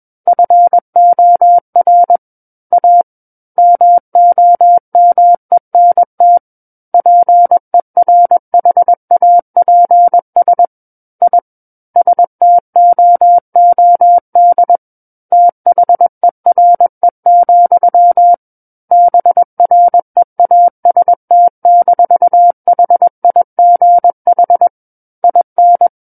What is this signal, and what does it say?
Morse code, heard as a keyed tone, about 21 wpm, 700 Hz. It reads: FOR A MOMENT PERHAPS I STOOD THERE, BREAST-HIGH IN
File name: Book One / 12-Chapter 12 - 21 WPM